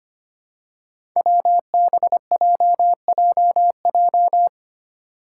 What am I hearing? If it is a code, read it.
Morse code, heard as a keyed tone, about 25 wpm, 700 Hz. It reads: WBJJJ
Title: Five Challenging Letter-Number Combinations 25wpm